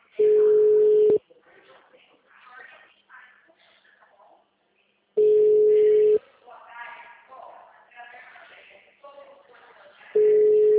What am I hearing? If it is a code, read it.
Ringback tone (Europe, 425 Hz), with speech.
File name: out- -2004- - - .141